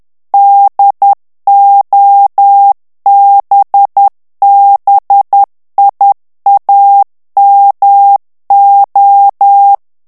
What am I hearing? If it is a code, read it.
Morse code, heard as a keyed tone, about 10 wpm, 800 Hz. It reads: DOBBIAMO
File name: morse mp3 / ITALIEN texte clair 1 vitesse 10 mots minute